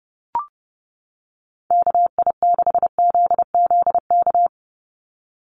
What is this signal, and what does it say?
Morse code, heard as a keyed tone, about 30 wpm, 700 Hz. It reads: KI6ZZK
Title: USA Call Signs 30wpm